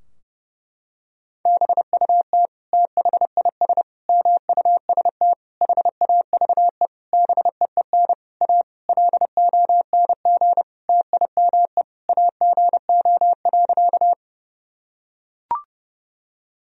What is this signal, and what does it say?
Morse code, heard as a keyed tone, about 30 wpm, 700 Hz. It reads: BUT THIS MUST HAVE BEEN A LONG TIME AGO.
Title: Sentences from Top 1000 Words - Review Part 3 - 30wpm